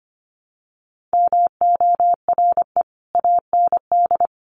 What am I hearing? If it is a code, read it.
Morse code, heard as a keyed tone, about 25 wpm, 700 Hz. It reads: MORE AND
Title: Sets of 3 Words 25wpm